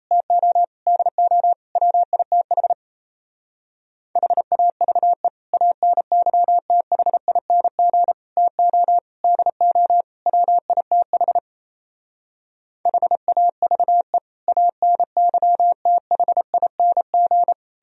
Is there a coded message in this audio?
Yes: Morse code from 0.1 to 17.5 s, heard as a keyed tone, about 38 wpm, 700 Hz. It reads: TO DO WITH HAVE ANYTHING TO DO WITH HAVE ANYTHING